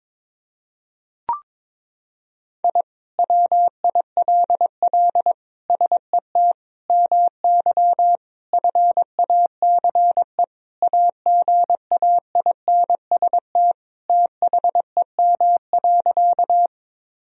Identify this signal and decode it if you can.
Morse code, heard as a keyed tone, about 22 wpm, 700 Hz. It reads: I WILL SET MY FACE AGAINST THEM.
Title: Sentences from Top 1000 Words - Review Part 2 - 22wpm